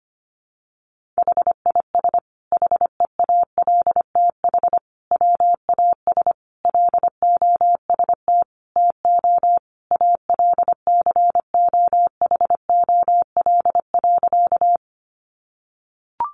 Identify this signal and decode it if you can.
Morse code, heard as a keyed tone, about 25 wpm, 700 Hz. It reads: HIS HEALTH WAS LOST TO ALCOHOL.
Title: Sentences from Top 1400 Words 25wpm